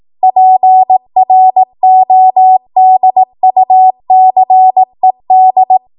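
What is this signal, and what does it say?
Morse code, heard as a keyed tone, about 18 wpm, 750 Hz. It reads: PRODUCED
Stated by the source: ARRL 18 wpm Morse code practice archive transmission.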